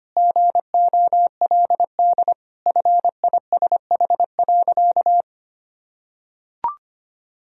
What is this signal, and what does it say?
Morse code, heard as a keyed tone, about 25 wpm, 700 Hz. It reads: GOLD FISH.
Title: Sentences from Top 1000 Words - Review Part 3 - 25wpm